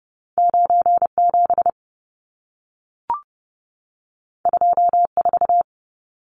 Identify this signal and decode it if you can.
Morse code, heard as a keyed tone, about 30 wpm, 700 Hz. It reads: 97 24
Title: Numbers - 2-Digit 30wpm